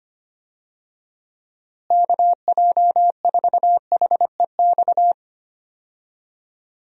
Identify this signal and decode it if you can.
Morse code, heard as a keyed tone, about 25 wpm, 700 Hz. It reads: KJ4HEX
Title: USA Call Signs - 6 Characters - Repeated 2x - 25wpm